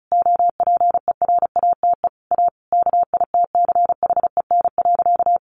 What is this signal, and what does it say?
Morse code, heard as a keyed tone, about 34 wpm, 700 Hz. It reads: OPERATE A KITCHEN.
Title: Sentences from Top 1500 Words 35wpm